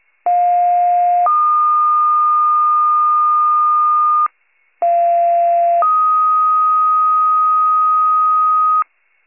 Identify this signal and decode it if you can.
Two-tone page, heard as a one-second tone, then a three-second tone, about 710 then 1220 Hz, about 690 then 1250 Hz. It is heard from 0.3 to 8.8 s.
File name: pager / 21 FIRE / wcalert